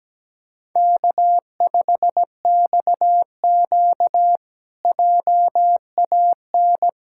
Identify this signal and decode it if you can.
Morse code, heard as a keyed tone, about 17 wpm, 700 Hz. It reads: K5XQ JAN
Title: K1USN SST QSO Practice - 17wpm